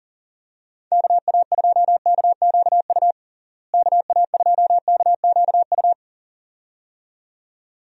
Morse code, heard as a keyed tone, about 38 wpm, 700 Hz. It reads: KA2KQU KA2KQU
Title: USA Call Signs - 6 Characters - Repeated 2x - 40wpm